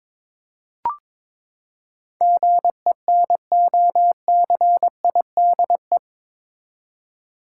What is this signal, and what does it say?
Morse code, heard as a keyed tone, about 21 wpm, 700 Hz. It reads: GENOCIDE
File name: practice / 22wpm / Eight Letter Words - 22wpm